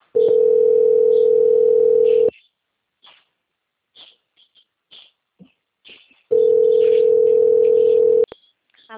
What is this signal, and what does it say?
Ringback tone (North America), 440 + 480 Hz.